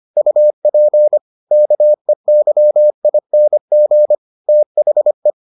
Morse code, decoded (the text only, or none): UP KEYING THE